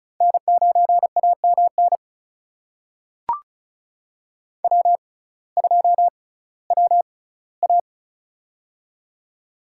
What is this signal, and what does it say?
Morse code, heard as a keyed tone, about 36 wpm, spaced as for 19 wpm, 700 Hz. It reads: N9AMN W2WA
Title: USA Call Signs - 6x Character Spacing - 35wpm